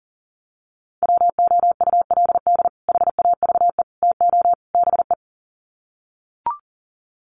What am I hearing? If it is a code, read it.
Morse code, heard as a keyed tone, about 40 wpm, 700 Hz. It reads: WOULD HAVE TO BE
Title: Sets of 4 Words 40wpm